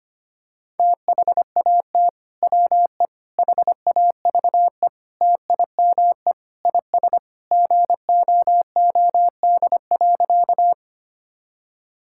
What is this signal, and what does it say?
Morse code, heard as a keyed tone, about 25 wpm, 700 Hz. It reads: THAT WE HAVE TIME IS GOOD.
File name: practice / 25wpm / Sentences with 3-Word Phrases - 25wpm